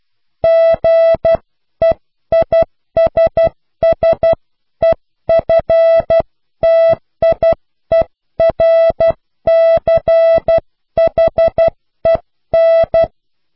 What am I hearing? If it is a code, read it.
Morse code, heard as a keyed tone, about 12 wpm, 660 Hz. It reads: GEISSEFTIERCHEN
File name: cw / Pseudo woorden (15)